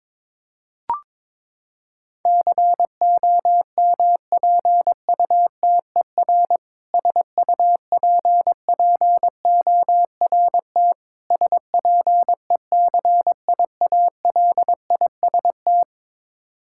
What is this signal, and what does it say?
Morse code, heard as a keyed tone, about 22 wpm, 700 Hz. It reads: COMPUTER SUPPORT SPECIALIST
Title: K1USN SST QSO Practice 22wpm